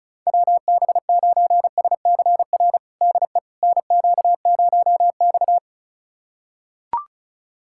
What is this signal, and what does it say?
Morse code, heard as a keyed tone, about 36 wpm, 700 Hz. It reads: WB9SCR DE NQ0X
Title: USA Callsign DE Callsign 35wpm